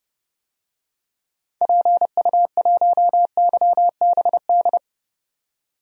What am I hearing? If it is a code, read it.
Morse code, heard as a keyed tone, about 30 wpm, 700 Hz. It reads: PU1YBD
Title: DXCC-100 Call Signs - 6 Characters - 30wpm